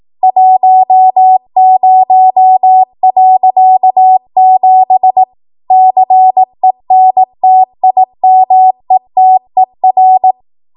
Morse code, heard as a keyed tone, about 18 wpm, 750 Hz. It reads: 10.7 CENTIMETER
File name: Morse / 18WPM